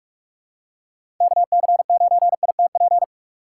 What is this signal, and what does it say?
Morse code, heard as a keyed tone, about 46 wpm, 700 Hz. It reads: KC9ITP
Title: USA Call Signs - 6 Characters - 4x Character Spacing - 45wpm